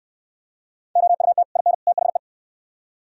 Morse code, heard as a keyed tone, about 69 wpm, 700 Hz. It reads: OUT IN THE